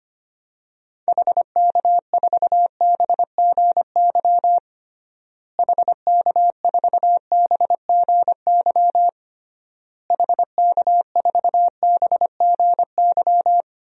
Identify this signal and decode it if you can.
Morse code, heard as a keyed tone, about 25 wpm, 700 Hz. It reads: HK4BGY HK4BGY HK4BGY
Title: DXCC-100 Call Signs - 6 Characters - Repeated 3x - 25wpm